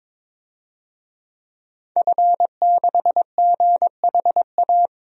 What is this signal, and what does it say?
Morse code, heard as a keyed tone, about 21 wpm, 700 Hz. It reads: F6GHA
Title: DXCC-100 Call Signs - 5 Characters - Repeated 2x - 22wpm